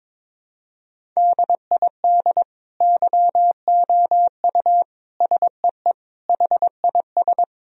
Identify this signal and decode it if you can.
Morse code, heard as a keyed tone, about 22 wpm, 700 Hz. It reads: DID YOU SEE HIS
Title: Sentences from Top 1000 Words - Review Part 1 - 22wpm